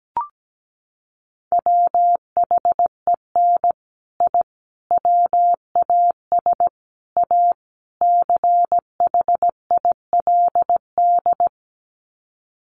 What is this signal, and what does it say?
Morse code, heard as a keyed tone, about 17 wpm, 690 Hz. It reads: WHEN I WAS A CHILD
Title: Sets of 5 Words Encore 17wpm